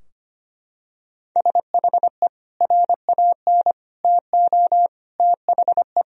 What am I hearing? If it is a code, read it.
Morse code, heard as a keyed tone, about 25 wpm, 700 Hz. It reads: SHE RAN TO THE
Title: Sentences from Top 1000 Words - Review Part 3 - 25wpm